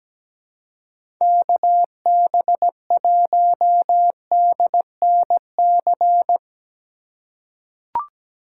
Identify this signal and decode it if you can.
Morse code, heard as a keyed tone, about 17 wpm, 700 Hz. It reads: KB1DNC
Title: USA Call Signs - 6x Character Spacing - 17wpm